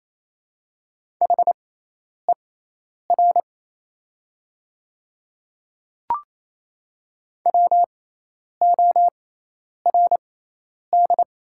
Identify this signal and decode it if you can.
Morse code, heard as a keyed tone, about 27 wpm, spaced as for 10 wpm, 700 Hz. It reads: HER WORD
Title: 6x Character Spacing - Top 100 Words - 28wpm